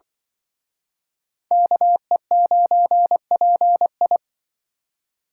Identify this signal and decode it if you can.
Morse code, heard as a keyed tone, about 25 wpm, 700 Hz. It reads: KE9PI
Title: Speed Racing - USA 2x2 Call Signs 20wpm